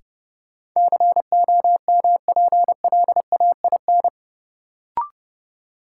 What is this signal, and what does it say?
Morse code, heard as a keyed tone, about 30 wpm, 700 Hz. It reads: COMPLAIN
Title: Sentences from Top 1300 Words - New Word Drill Alt - 30wpm